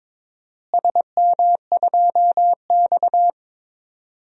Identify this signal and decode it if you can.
Morse code, heard as a keyed tone, about 21 wpm, 700 Hz. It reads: SM2X
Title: DXCC-100 Call Signs - 4 Characters - Repeated 3x - 22wpm